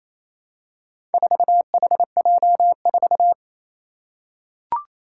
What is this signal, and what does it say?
Morse code, heard as a keyed tone, about 29 wpm, 700 Hz. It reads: 4HJ4
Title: Four Challenging Letter-Number Combinations 28wpm